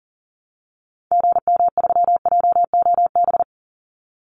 Morse code, heard as a keyed tone, about 40 wpm, 700 Hz. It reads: GM3JOB
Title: Call Signs Repeated 2x - 40wpm